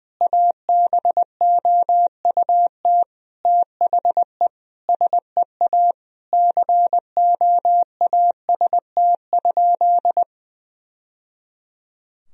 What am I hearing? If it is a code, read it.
Morse code, heard as a keyed tone, about 20 wpm, 700 Hz. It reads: ABOUT THE SEA COAST?
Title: Sentences from Top 1000 Words - Review Part 4 - 20wpm